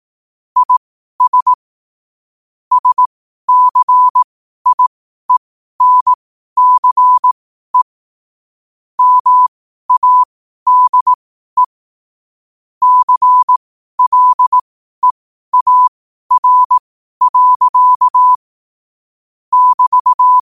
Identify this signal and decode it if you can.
Morse code, heard as a keyed tone, about 18 wpm, spaced as for 12 wpm, 1000 Hz. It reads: IS SCIENCE MADE CLEAR. <BT>
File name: qotdfiles / 12wpm